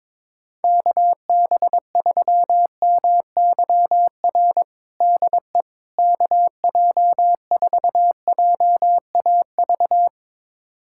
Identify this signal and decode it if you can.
Morse code, heard as a keyed tone, about 22 wpm, 700 Hz. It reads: KB3MYR DE KJ4JAV
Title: USA Callsign DE Callsign - 6 Characters - 22wpm